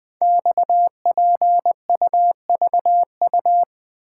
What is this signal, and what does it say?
Morse code, heard as a keyed tone, about 20 wpm, 700 Hz. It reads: XPUVU
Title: Code Groups - 5 Letters - 20wpm